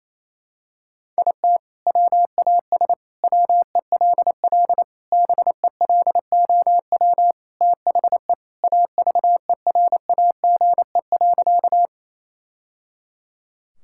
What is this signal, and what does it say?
Morse code, heard as a keyed tone, about 27 wpm, 700 Hz. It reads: IT WAS WELL BELOW THE AVERAGE.
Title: Sentences from Top 1000 Words - Review Part 6 - 28wpm